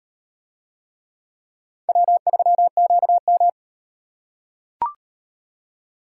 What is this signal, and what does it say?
Morse code, heard as a keyed tone, about 38 wpm, 700 Hz. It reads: W3QM